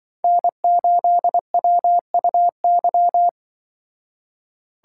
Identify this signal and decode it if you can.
Morse code, heard as a keyed tone, about 25 wpm, 700 Hz. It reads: N8WUY
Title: Speed Racing - USA Call Signs 20wpm